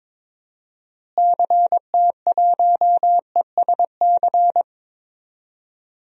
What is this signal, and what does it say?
Morse code, heard as a keyed tone, about 22 wpm, 700 Hz. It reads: CT1ESC
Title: DXCC-100 Call Signs - 6 Characters - Repeated 2x - 22wpm